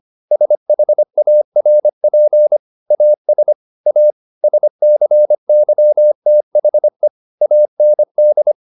Morse code, heard as a keyed tone, about 25 wpm, 600 Hz. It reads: SHARP AS A SCYTHE AND
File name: the old man and the sea part 1 25wpm